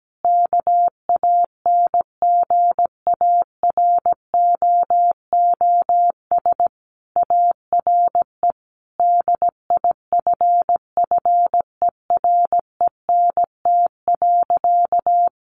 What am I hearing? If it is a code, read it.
Morse code, heard as a keyed tone, about 17 wpm, 700 Hz. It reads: KANGAROOS ARE DIFFERENT.
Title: Easy Phrases 17wpm